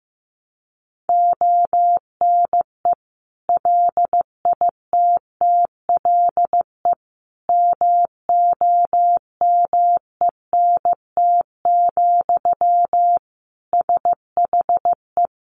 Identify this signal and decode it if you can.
Morse code, heard as a keyed tone, about 15 wpm, 700 Hz. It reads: ONE LITTLE MOMENT, SHE
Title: Sentences from Top 1000 Words 15wpm